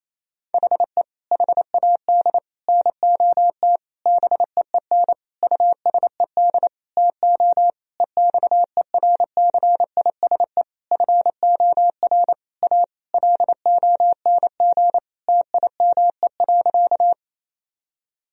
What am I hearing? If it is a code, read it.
Morse code, heard as a keyed tone, about 28 wpm, 700 Hz. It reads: HE HAD NOT BEEN USED TO EXERCISE FOR A LONG TIME.
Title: Sentences from Top 1000 Words - Review Part 4 - 28wpm